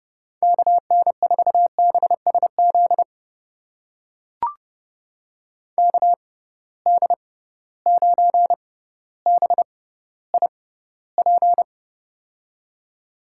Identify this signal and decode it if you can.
Morse code, heard as a keyed tone, about 30 wpm, spaced as for 15 wpm, 700 Hz. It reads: KN4BSZ KD9BIP
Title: USA Call Signs - 6 Characters - 6x Character Spacing - 30wpm